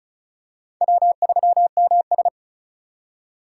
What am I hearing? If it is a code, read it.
Morse code, heard as a keyed tone, about 36 wpm, 700 Hz. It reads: W3MS
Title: K1USN SST QSO Practice - 35wpm